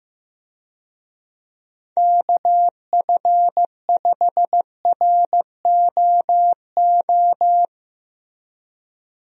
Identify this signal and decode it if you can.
Morse code, heard as a keyed tone, about 15 wpm, 700 Hz. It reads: KF5ROO
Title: USA Call Signs - 6 Characters - Repeated 3x - 15wpm